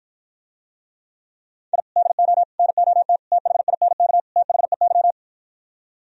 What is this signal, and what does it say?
Morse code, heard as a keyed tone, about 52 wpm, 700 Hz. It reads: I DO NOT THINK THEY